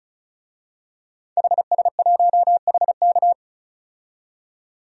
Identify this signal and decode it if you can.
Morse code, heard as a keyed tone, about 36 wpm, 700 Hz. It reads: HS1HK
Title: DXCC-100 Call Signs - 5 Characters - 35wpm